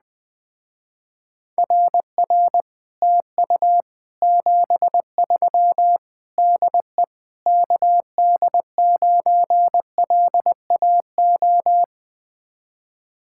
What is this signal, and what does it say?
Morse code, heard as a keyed tone, about 20 wpm, 700 Hz. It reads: RR TU 73 DE KD9LAO
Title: Speed-Racing Common QSO Phrases Encore - 17wpm